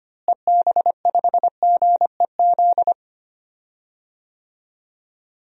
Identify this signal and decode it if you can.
Morse code, heard as a keyed tone, about 25 wpm, 700 Hz. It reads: EB5GEZ
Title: DXCC-100 Call Signs - 6 Characters - Repeated 2x - 25wpm